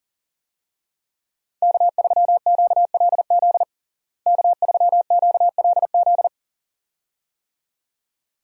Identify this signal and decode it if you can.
Morse code, heard as a keyed tone, about 40 wpm, 700 Hz. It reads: K3QLZ K3QLZ
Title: USA Call Signs - 5 Characters - Repeated 2x - 40wpm